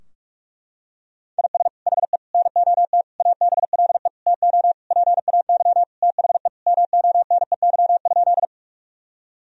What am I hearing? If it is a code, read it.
Morse code, heard as a keyed tone, about 41 wpm, 700 Hz. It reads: IS HE NOT ABLE TO PAY THE MONEY?
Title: Sentences from Top 1000 Words - Review Part 2 - 45wpm